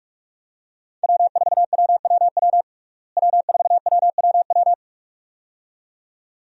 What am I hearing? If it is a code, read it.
Morse code, heard as a keyed tone, about 46 wpm, 700 Hz. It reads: W4WWW W4WWW
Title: USA Call Signs - 5 Characters - Repeated 2x - 45wpm